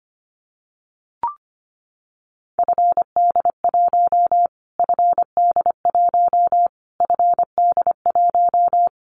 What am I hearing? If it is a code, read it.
Morse code, heard as a keyed tone, about 25 wpm, 700 Hz. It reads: FD1 FD1 FD1
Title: Callsign Prefixes for Top 25 DXCC - Repeated 3x - 25wpm